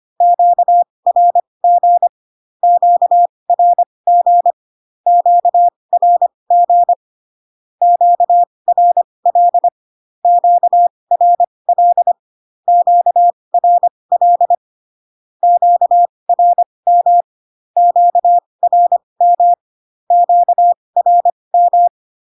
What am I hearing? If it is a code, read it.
Morse code, heard as a keyed tone, about 25 wpm, spaced as for 20 wpm, 700 Hz. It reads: QRG QRG QRG QRL QRL QRL QRM QRM QRM
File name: cw / Q-codes most common repeated 3 times 15 25wpm ik1hge